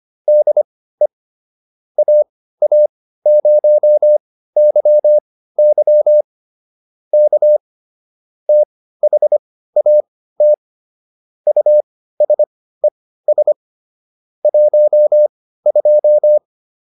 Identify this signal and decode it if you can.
Morse code, heard as a keyed tone, about 25 wpm, spaced as for 15 wpm, 600 Hz. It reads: DE AA0YY K THAT USES 12